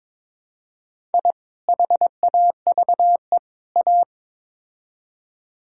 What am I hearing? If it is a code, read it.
Morse code, heard as a keyed tone, about 22 wpm, 700 Hz. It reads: I HAVE A
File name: practice / 22wpm / Build A Phrase - 5-Word Phrases - 22wpm